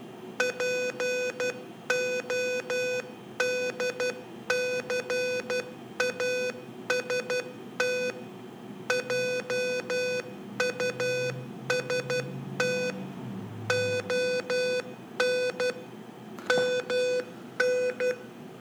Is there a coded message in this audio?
Yes: Morse code from 0.4 to 18.1 s, heard as a keyed tone, about 12 wpm, 500 Hz. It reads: PODCAST JUST ON MN